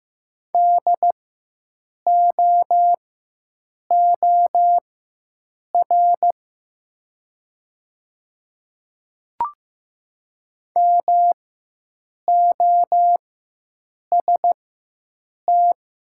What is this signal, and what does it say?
Morse code, heard as a keyed tone, about 15 wpm, spaced as for 7 wpm, 700 Hz. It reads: DOOR MOST
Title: 4x Character Spacing - Top 300 Words - 15wpm